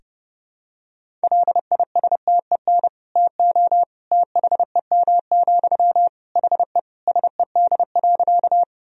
Morse code, heard as a keyed tone, about 32 wpm, 700 Hz. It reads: LISTEN TO THEM, HE SED.
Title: Sentences from Top 400 Words and Abbreviations 30wpm